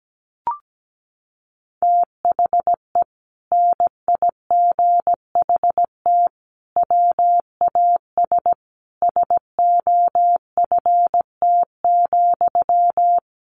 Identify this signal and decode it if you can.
Morse code, heard as a keyed tone, about 17 wpm, 700 Hz. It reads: THE NIGHT WAS SOFT,
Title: Sentences from Top 1500 Words 17wpm